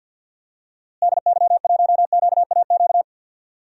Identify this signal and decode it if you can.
Morse code, heard as a keyed tone, about 52 wpm, 700 Hz. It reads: DY1QAQ
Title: DXCC-100 Call Signs - 6 Characters - Repeated 2x - 50wpm